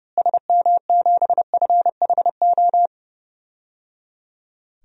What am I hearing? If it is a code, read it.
Morse code, heard as a keyed tone, about 30 wpm, 700 Hz. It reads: SM7FHO
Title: DXCC-100 Call Signs - 6 Characters - Repeated 3x - 30wpm